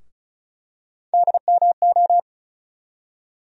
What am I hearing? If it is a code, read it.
Morse code, heard as a keyed tone, about 36 wpm, 700 Hz. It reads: DMO
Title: USA Call Sign Postfixes - 3 Letters 35wpm